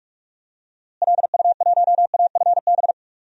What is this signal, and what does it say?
Morse code, heard as a keyed tone, about 42 wpm, 700 Hz. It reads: LU1AFB
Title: DXCC-100 Call Signs - 6 Characters - Repeated 3x - 45wpm